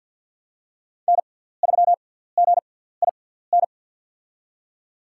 Morse code, heard as a keyed tone, about 51 wpm, spaced as for 17 wpm, 700 Hz. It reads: N3GIN
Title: USA Call Signs - 6x Character Spacing - 50wpm